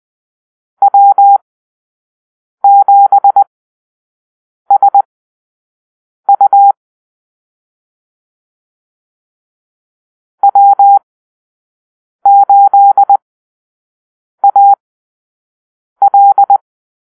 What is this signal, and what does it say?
Morse code, heard as a keyed tone, about 20 wpm, spaced as for 6 wpm, 800 Hz. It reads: W7SU W8AL
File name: arrl / Call-Sign Drill-5